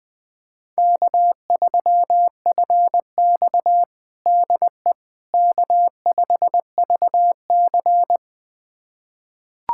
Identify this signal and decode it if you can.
Morse code, heard as a keyed tone, about 21 wpm, 700 Hz. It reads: K3FX DE K5VC
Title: USA Callsign DE Callsign - 4 Characters - 20wpm